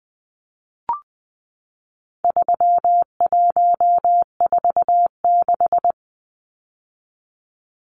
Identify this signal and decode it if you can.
Morse code, heard as a keyed tone, about 20 wpm, 700 Hz. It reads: 3146
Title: Numbers - 4-Digit 20wpm